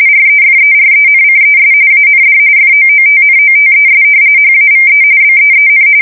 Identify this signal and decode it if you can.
RTTY, about 45.45 baud. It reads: REFERENCES AND FOOTNOTES HAVE BEEN
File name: fsk170 / usnavy1